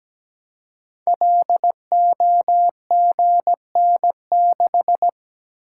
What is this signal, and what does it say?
Morse code, heard as a keyed tone, about 17 wpm, 700 Hz. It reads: LOGN6
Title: Code Groups - 5 Letter:Numbers - 17wpm